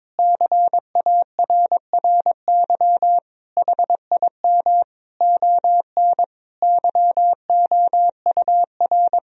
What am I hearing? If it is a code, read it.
Morse code, heard as a keyed tone, about 21 wpm, 700 Hz. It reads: CARRY HIM ON YOUR
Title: Sentences from Top 1000 Words - Review Part 1 - 22wpm